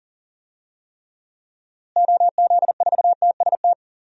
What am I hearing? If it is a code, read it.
Morse code, heard as a keyed tone, about 40 wpm, 700 Hz. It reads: OZ4TST
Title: DXCC-100 Call Signs - 6 Characters - Repeated 3x - 40wpm